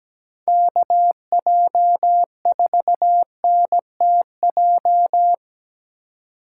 Morse code, heard as a keyed tone, about 17 wpm, 700 Hz. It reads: KJ4NTJ
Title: USA Call Signs - 6 Characters - 4x Character Spacing - 17wpm